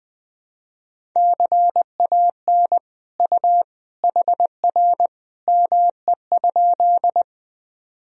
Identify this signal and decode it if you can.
Morse code, heard as a keyed tone, about 21 wpm, 700 Hz. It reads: CAN U HR ME?
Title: Sentences from Top 300 Words and Abbreviations 20wpm